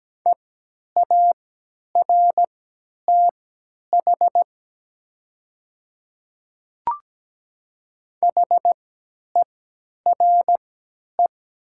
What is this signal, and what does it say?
Morse code, heard as a keyed tone, about 17 wpm, spaced as for 10 wpm, 700 Hz. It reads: EARTH HERE
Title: 3x Character Spacing - Top 200 Words - 17wpm